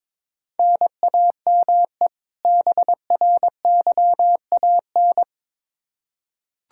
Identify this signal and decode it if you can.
Morse code, heard as a keyed tone, about 22 wpm, 700 Hz. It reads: NAME BRYAN
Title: K1USN SST QSO Practice 22wpm